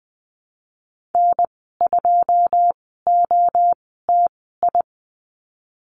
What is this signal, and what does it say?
Morse code, heard as a keyed tone, about 19 wpm, spaced as for 14 wpm, 700 Hz. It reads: N2OTI
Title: 088 Lesson 15g - Number 5 - Call Signs 20wpm